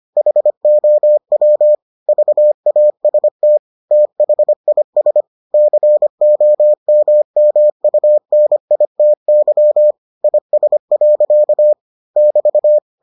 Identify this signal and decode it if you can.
Morse code, heard as a keyed tone, about 25 wpm, 600 Hz. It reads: HOW VAST THIS COMMUNITY IS. <BT>